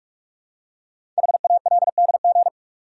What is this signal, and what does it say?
Morse code, heard as a keyed tone, about 44 wpm, 690 Hz. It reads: HALDG
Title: Code Groups - 5 Letters - 45wpm